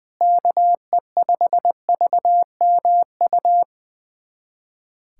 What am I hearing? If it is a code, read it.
Morse code, heard as a keyed tone, about 21 wpm, 700 Hz. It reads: KE5VMU
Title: USA Callsign DE Callsign - 6 Characters - 20wpm